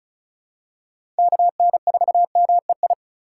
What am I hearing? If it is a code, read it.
Morse code, heard as a keyed tone, about 36 wpm, 700 Hz. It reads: KN4MEI
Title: USA Call Signs - 6 Characters - 4x Character Spacing - 35wpm